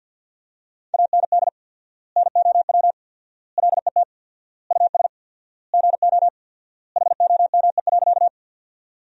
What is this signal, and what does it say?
Morse code, heard as a keyed tone, about 50 wpm, 700 Hz. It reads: AND NOW LET US GO HOME.
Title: 4x Word-Spacing - Sentences from Top 300 Words - 50wpm